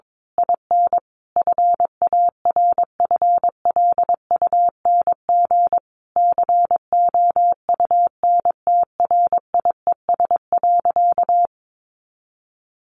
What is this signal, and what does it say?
Morse code, heard as a keyed tone, about 22 wpm, 700 Hz. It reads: IN FARFLUNG COUNTRIES.
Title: A Sound Of Thunder 22wpm